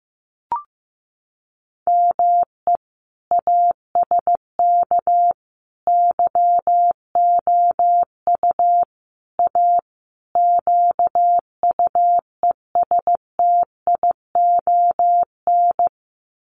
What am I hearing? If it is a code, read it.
Morse code, heard as a keyed tone, about 15 wpm, 700 Hz. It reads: ME ASK YOU A QUESTION
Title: Sets of 5 Words Encore2 15wpm